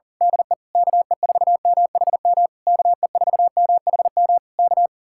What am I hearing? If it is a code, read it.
Morse code, heard as a keyed tone, about 35 wpm, 700 Hz. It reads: DE KE4MHM KE4MHM K
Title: Speed-Racing Common QSO Phrases Encore - 40wpm